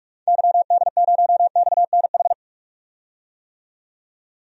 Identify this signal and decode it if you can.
Morse code, heard as a keyed tone, about 46 wpm, 700 Hz. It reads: YD0XNH
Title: DXCC-100 Call Signs - 6 Characters - 45wpm